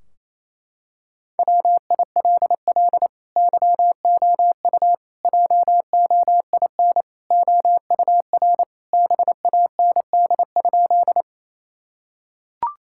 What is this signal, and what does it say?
Morse code, heard as a keyed tone, about 28 wpm, 700 Hz. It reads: WILL YOU JOIN OUR BAND?
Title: Sentences from Top 1600 Words 28wpm